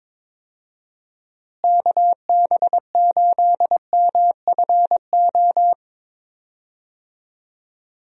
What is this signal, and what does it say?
Morse code, heard as a keyed tone, about 22 wpm, 700 Hz. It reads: KB8MFO
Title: USA Call Signs - 6 Characters - Repeated 2x - 22wpm